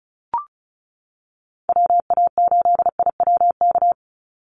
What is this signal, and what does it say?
Morse code, heard as a keyed tone, about 34 wpm, 700 Hz. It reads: WA8IWK
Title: Call Signs - 35wpm